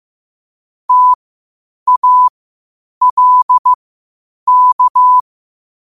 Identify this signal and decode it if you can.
Morse code, heard as a keyed tone, about 15 wpm, spaced as for 8 wpm, 1000 Hz. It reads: TALK
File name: qotdfiles / 8wpm